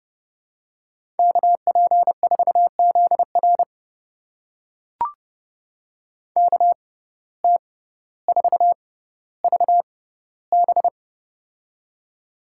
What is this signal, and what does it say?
Morse code, heard as a keyed tone, about 30 wpm, spaced as for 15 wpm, 700 Hz. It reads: KP4ZR KT4VB
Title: USA Call Signs - 5 Characters - 6x Character Spacing - 30wpm